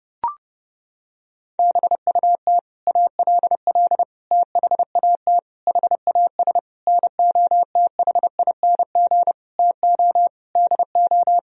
Morse code, heard as a keyed tone, about 30 wpm, 700 Hz. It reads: BUT ALL THAT HAS NOTHING TO DO
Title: Sentences from Top 1000 Words - Review Part 3 - 30wpm